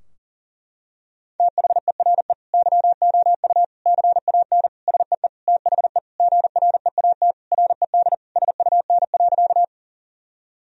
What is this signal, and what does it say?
Morse code, heard as a keyed tone, about 40 wpm, 700 Hz. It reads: THERE YOU CAN SEE THE GREAT RED SUN.
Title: Sentences from Top 1000 Words - Review Part 2 - 40wpm